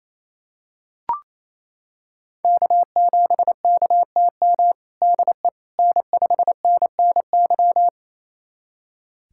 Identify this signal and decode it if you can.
Morse code, heard as a keyed tone, about 28 wpm, 700 Hz. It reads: K7KTM DE N5NNY
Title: USA Callsign DE Callsign 28wpm